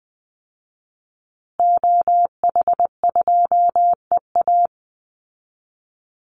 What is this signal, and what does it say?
Morse code, heard as a keyed tone, about 20 wpm, 700 Hz. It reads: OH2EA
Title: Call Signs Repeated 3x - 20wpm